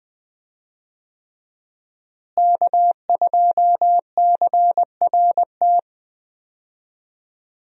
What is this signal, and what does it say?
Morse code, heard as a keyed tone, about 20 wpm, 700 Hz. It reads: K2CRT
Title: USA Call Signs - 5 Characters - Repeated 3x - 20wpm